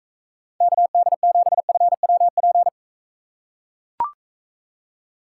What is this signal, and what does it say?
Morse code, heard as a keyed tone, about 42 wpm, 700 Hz. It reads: KD7FWP